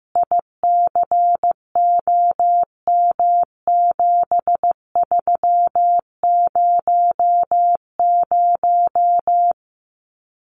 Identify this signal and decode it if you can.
Morse code, heard as a keyed tone, about 15 wpm, 700 Hz. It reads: ICOM7300
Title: Common QSO Phrases 15wpm